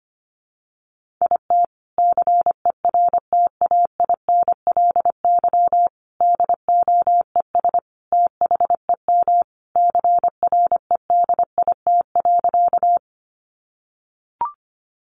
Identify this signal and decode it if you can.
Morse code, heard as a keyed tone, about 25 wpm, 700 Hz. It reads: IT CERTAINLY DOES THEM CREDIT.
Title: Sentences from Top 1000 Words 25wpm